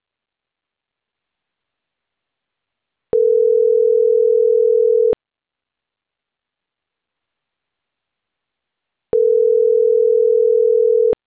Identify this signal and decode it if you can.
Ringback tone (North America), 440 + 480 Hz.